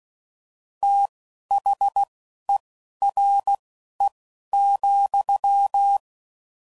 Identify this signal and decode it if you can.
Morse code, heard as a keyed tone, about 16 wpm, spaced as for 11 wpm, 800 Hz. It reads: THERE,
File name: Prefixes / FLs & THs / Fast TH Prefixes